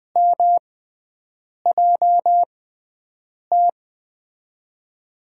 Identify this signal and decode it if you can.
Morse code, heard as a keyed tone, about 20 wpm, spaced as for 7 wpm, 700 Hz. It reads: MJT
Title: USA Call Signs - 6 Characters - 6x Character Spacing - 20wpm